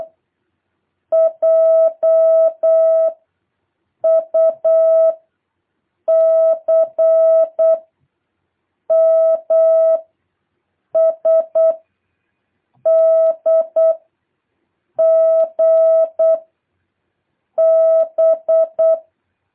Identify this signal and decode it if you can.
Morse code, heard as a keyed tone, about 8 wpm, 650 Hz. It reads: J U C M S D G B
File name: jh8cbh / 2023cw